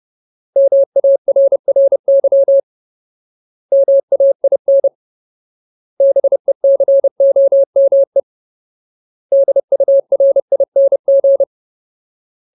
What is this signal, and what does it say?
Morse code, heard as a keyed tone, about 30 wpm, 550 Hz. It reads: MARRY MAIN BECOME DURING
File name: Ultimate Word List / 30WPM Ultimate